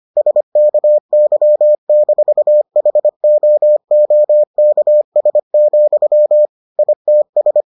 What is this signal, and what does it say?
Morse code, heard as a keyed tone, about 25 wpm, 600 Hz. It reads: SKY-HOOKS, ITS